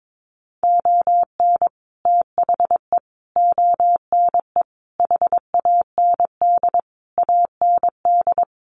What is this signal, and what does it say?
Morse code, heard as a keyed tone, about 22 wpm, 700 Hz. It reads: ON THE ONE HAND AND
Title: Sets of 5 Words Encore2 22wpm